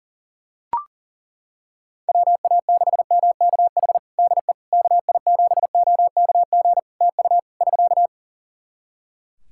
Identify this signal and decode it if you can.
Morse code, heard as a keyed tone, about 40 wpm, 700 Hz. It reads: WA6MKH DE KI7OKG TU <SK>
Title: Practice QSO - Exchange 4 - Ending - 40wpm